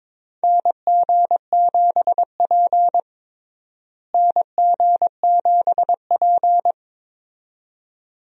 Morse code, heard as a keyed tone, about 22 wpm, 700 Hz. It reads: NG7P NG7P
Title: USA Call Signs - 4 Characters - Repeated 2x - 22wpm